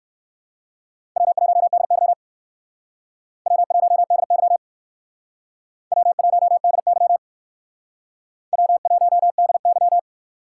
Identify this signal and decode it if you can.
Morse code, heard as a keyed tone, about 65 wpm, 700 Hz. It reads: W1DY W1DY W1DY W1DY